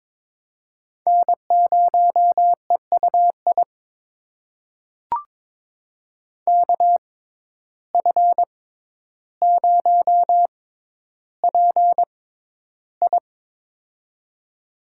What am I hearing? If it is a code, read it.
Morse code, heard as a keyed tone, about 21 wpm, spaced as for 11 wpm, 700 Hz. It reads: N0EUI KF0PI
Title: USA Call Signs - 5 Characters - 6x Character Spacing - 22wpm